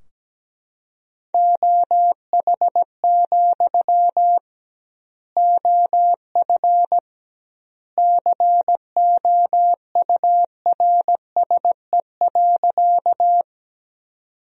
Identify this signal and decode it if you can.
Morse code, heard as a keyed tone, about 17 wpm, 700 Hz. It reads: OH, OF COURSE.
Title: 2x Word-Spacing - Sentences from Top 500 Words - 17wpm